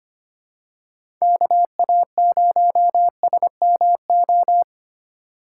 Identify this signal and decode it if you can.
Morse code, heard as a keyed tone, about 25 wpm, 700 Hz. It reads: KA0SMO
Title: USA Call Signs - 6 Characters - 2x Character Spacing - 25wpm